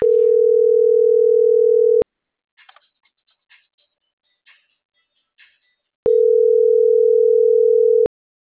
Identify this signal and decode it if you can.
Ringback tone (North America), 440 + 480 Hz.